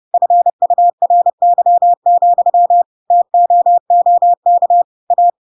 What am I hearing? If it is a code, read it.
Morse code, heard as a keyed tone, about 30 wpm, 700 Hz. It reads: FURY, TOOK A
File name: Pinocchio / Pinocchio ch4 30 30 ik1hge